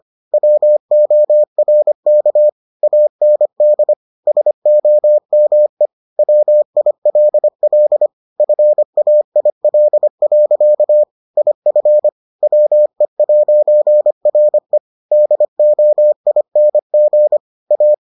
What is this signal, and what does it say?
Morse code, heard as a keyed tone, about 25 wpm, 600 Hz. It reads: WORK AND SOME WILL FAIL. IF WE'RE DOING A